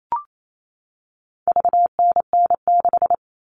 Morse code, heard as a keyed tone, about 28 wpm, 700 Hz. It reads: VNN6
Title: All Cut Numbers - 4-Digit 28wpm